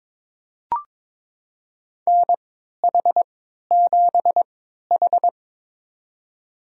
Morse code, heard as a keyed tone, about 22 wpm, spaced as for 13 wpm, 700 Hz. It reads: NH7H
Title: USA Call Signs - 4 Characters - 3x Character Spacing - 22wpm